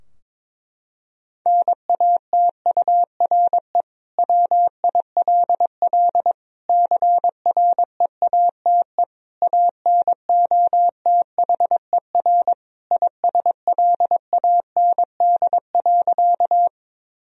Morse code, heard as a keyed tone, about 22 wpm, 700 Hz. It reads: NATURE WILL CREATE ANOTHER ISLAND.
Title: Sentences from Top 1000 Words - Review Part 3 - 22wpm